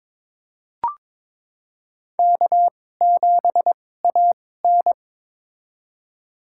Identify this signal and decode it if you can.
Morse code, heard as a keyed tone, about 21 wpm, spaced as for 16 wpm, 700 Hz. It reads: K7AN
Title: USA Call Signs - 4 Characters - 2x Character Spacing - 22wpm